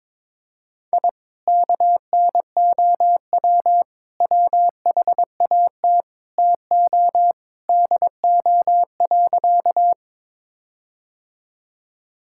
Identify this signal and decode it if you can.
Morse code, heard as a keyed tone, about 22 wpm, 700 Hz. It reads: I KNOW WHAT TO DO.
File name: practice / 22wpm / Sentences with 4-Word Phrases - 22wpm